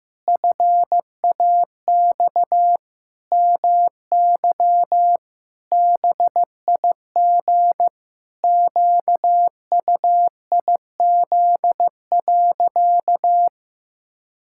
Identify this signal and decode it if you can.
Morse code, heard as a keyed tone, about 15 wpm, 700 Hz. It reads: FAX MY BIG QUIZ.
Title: Pangram Challenge v2 - 15wpm